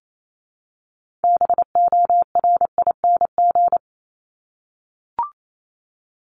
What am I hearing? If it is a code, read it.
Morse code, heard as a keyed tone, about 28 wpm, 700 Hz. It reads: BORING
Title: Top 1501-1750 Words 28wpm